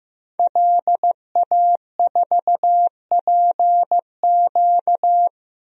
Morse code, heard as a keyed tone, about 15 wpm, 700 Hz. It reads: LA4PQ
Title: DXCC-100 Call Signs - 5 Characters - Repeated 2x - 15wpm